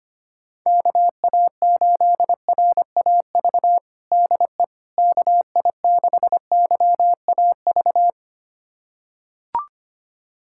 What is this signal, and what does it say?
Morse code, heard as a keyed tone, about 25 wpm, 700 Hz. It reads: KA8RAV DE KI6YAV
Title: USA Callsign DE Callsign - 6 Characters - 25wpm